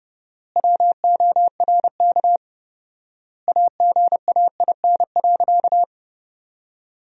Morse code, heard as a keyed tone, about 31 wpm, 700 Hz. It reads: WORK AGAIN.
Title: 4x Word-Spacing - Sentences from Top 200 Words - 30wpm